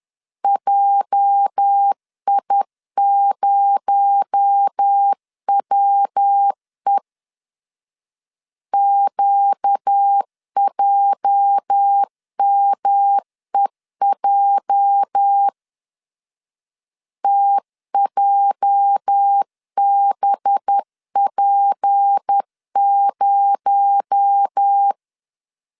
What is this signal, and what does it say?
Morse code, heard as a keyed tone, about 10 wpm, 800 Hz. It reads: JI0WE QJMEJ TJBP0
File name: LEZIONI / 8 YJ45 / 8 YJ 10wpm 2651